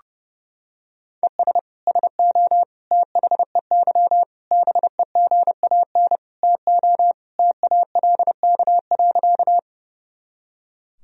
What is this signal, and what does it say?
Morse code, heard as a keyed tone, about 32 wpm, 700 Hz. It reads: ES SO THEY BEGAN TO TALK.
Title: Sentences from Top 400 Words and Abbreviations 30wpm